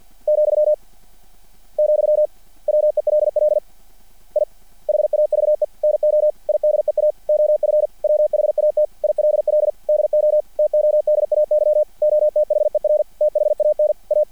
Morse code, heard as a keyed tone, about 50 wpm, 600 Hz. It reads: , , WELL I HAVE NO IDEA OF WHAT ILL DO TODAY OTHER THAN A
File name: archived / CRSnet